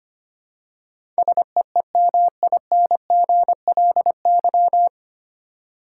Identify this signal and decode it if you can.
Morse code, heard as a keyed tone, about 25 wpm, 700 Hz. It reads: SEEMINGLY
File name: practice / 25wpm / 9 Letter Words - First 3 Letters Given - 25wpm